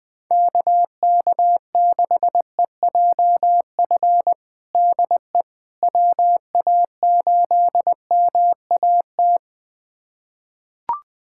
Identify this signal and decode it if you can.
Morse code, heard as a keyed tone, about 20 wpm, 700 Hz. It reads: KK6EJF DE WA8MAT
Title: USA Callsign DE Callsign 20wpm